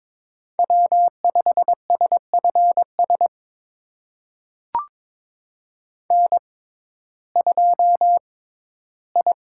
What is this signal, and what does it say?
Morse code, heard as a keyed tone, about 22 wpm, spaced as for 13 wpm, 700 Hz. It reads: W5SFS N2I
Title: USA Call Signs - 5 Characters - 6x Character Spacing - 22wpm